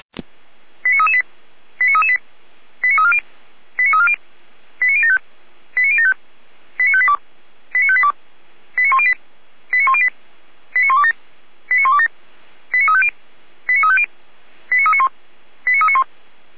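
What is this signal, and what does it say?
Five-tone selective call (ZVEI), heard as quick runs of five tones, tones of about 70 ms. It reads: 89338 89338 89340 89340 89075 89075 89582 89582 89108 89108 89127 89127 89350 89350 89391 89391
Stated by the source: ZVEI five-tone call received by SDR.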